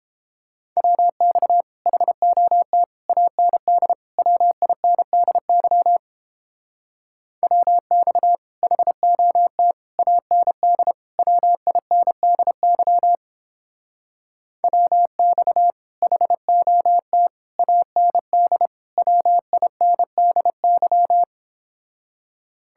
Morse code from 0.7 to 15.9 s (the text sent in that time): WX HOT AND WINDY WX HOT AND WINDY WX